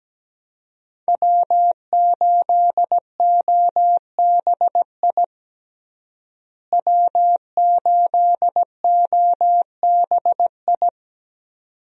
Morse code, heard as a keyed tone, about 17 wpm, 700 Hz. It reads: W8OBI W8OBI
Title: USA Call Signs - 5 Characters - Repeated 2x - 17wpm